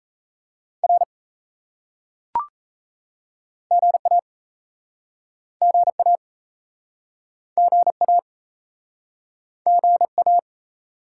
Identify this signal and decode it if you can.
Morse code, heard as a keyed tone, about 37 wpm, 700 Hz. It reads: R GA GA GA GA